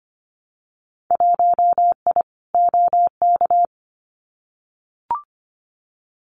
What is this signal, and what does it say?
Morse code, heard as a keyed tone, about 25 wpm, 700 Hz. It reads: 1I OK
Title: Winter Field Day Exchanges Encore 25wpm